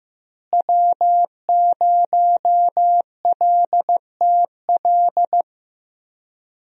Morse code, heard as a keyed tone, about 16 wpm, 700 Hz. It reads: W0LTL
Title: USA Call Signs - 5 Characters - Repeated 3x - 15wpm